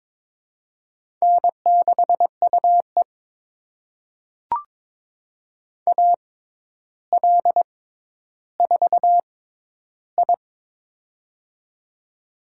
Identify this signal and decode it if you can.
Morse code, heard as a keyed tone, about 22 wpm, spaced as for 11 wpm, 700 Hz. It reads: N6UE AL4I
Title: USA Call Signs - 4 Characters - 6x Character Spacing - 22wpm